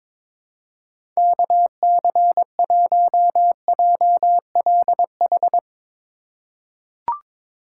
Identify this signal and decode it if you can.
Morse code, heard as a keyed tone, about 22 wpm, 700 Hz. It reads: KC1JLH
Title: USA Call Signs - 6 Characters - 6x Character Spacing - 22wpm